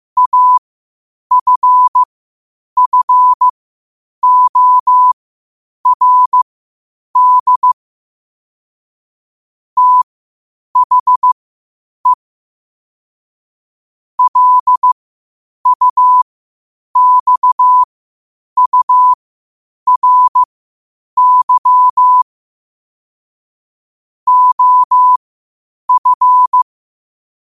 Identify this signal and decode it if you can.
Morse code, heard as a keyed tone, about 15 wpm, spaced as for 8 wpm, 1000 Hz. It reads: AFFORD THE LUXURY OF